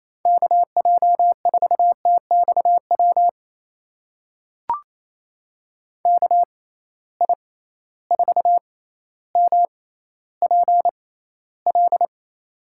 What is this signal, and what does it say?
Morse code, heard as a keyed tone, about 27 wpm, spaced as for 14 wpm, 700 Hz. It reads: KJ4TXW KI4MPL
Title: USA Call Signs - 6 Characters - 6x Character Spacing - 28wpm